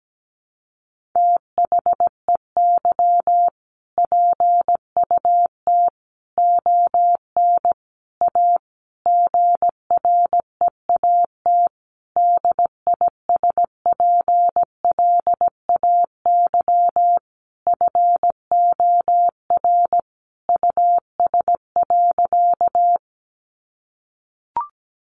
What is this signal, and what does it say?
Morse code, heard as a keyed tone, about 17 wpm, 700 Hz. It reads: THEY PUT ON A GREAT DISPLAY FOR US.
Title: Sentences from Top 1200 Words 17wpm